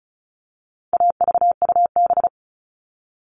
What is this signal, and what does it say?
Morse code, heard as a keyed tone, about 34 wpm, 700 Hz. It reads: AVUB
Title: All Cut Numbers - 4-Digit 35wpm